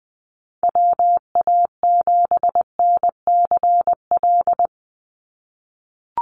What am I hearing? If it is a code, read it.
Morse code, heard as a keyed tone, about 20 wpm, 700 Hz. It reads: WA7NCL
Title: NA SOTA Activator Callsigns 20wpm